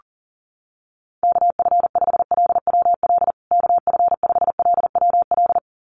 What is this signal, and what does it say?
Morse code, heard as a keyed tone, about 37 wpm, 700 Hz. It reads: KF5LWL KF5LWL
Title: USA Call Signs Repeated 2x 40wpm